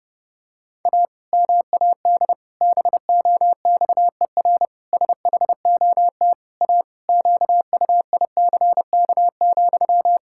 Morse code, heard as a keyed tone, about 30 wpm, 700 Hz. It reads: A MAD BOXER SHOT A QUICK,
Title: Pangram Challenge v2 - 30wpm